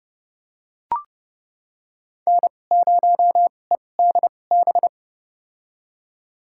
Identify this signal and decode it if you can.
Morse code, heard as a keyed tone, about 30 wpm, spaced as for 22 wpm, 700 Hz. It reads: N0EDB
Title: USA Call Signs - 5 Characters - 2x Character Spacing - 30wpm